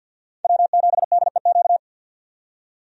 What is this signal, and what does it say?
Morse code, heard as a keyed tone, about 48 wpm, 700 Hz. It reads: W7DEQ
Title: USA Call Signs - 5 Characters - 50wpm